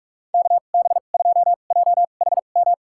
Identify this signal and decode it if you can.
Morse code, heard as a keyed tone, about 45 wpm, spaced as for 32 wpm, 700 Hz. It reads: KB2JHM
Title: USA Call Signs - 6 Characters - 2x Character Spacing - 45wpm